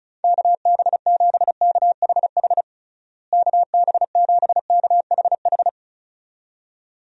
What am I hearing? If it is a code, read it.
Morse code, heard as a keyed tone, about 36 wpm, 700 Hz. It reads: KB7KHH KB7KHH
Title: USA Call Signs - 6 Characters - Repeated 2x - 35wpm